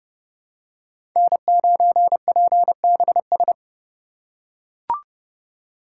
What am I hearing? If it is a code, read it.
Morse code, heard as a keyed tone, about 30 wpm, 700 Hz. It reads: N9PBS
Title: USA Call Signs - 5 Characters - 2x Character Spacing - 30wpm